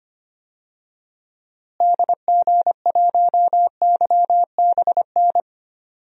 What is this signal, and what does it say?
Morse code, heard as a keyed tone, about 25 wpm, 700 Hz. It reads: DG1YBN
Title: DXCC-100 Call Signs - 6 Characters - Repeated 2x - 25wpm